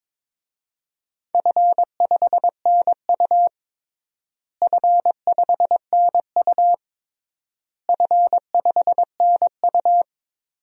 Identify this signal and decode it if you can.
Morse code, heard as a keyed tone, about 22 wpm, 700 Hz. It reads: F5NU F5NU F5NU
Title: DXCC-100 Call Signs - 4 Characters - Repeated 3x - 22wpm